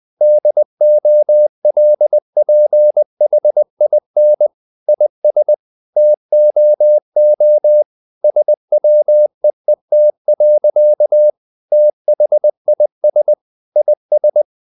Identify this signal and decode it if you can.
Morse code, heard as a keyed tone, about 20 wpm, 600 Hz. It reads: DOLPHIN IS TOO SWEET. THIS IS
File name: the old man and the sea / the old man and the sea part 1 20wpm